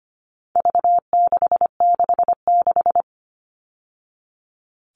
Morse code, heard as a keyed tone, about 25 wpm, 700 Hz. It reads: V666
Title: All Cut Numbers - 4-Digit 25wpm